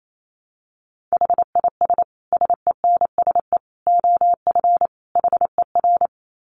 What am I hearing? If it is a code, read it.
Morse code, heard as a keyed tone, about 28 wpm, 700 Hz. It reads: HIS SENSE OF HER
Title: Dit Challenge 28wpm